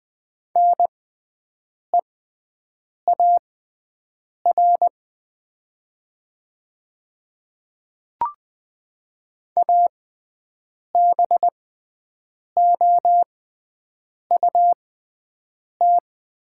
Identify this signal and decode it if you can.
Morse code, heard as a keyed tone, about 20 wpm, spaced as for 7 wpm, 700 Hz. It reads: NEAR ABOUT
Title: 6x Character Spacing - Top 500 Words - 20wpm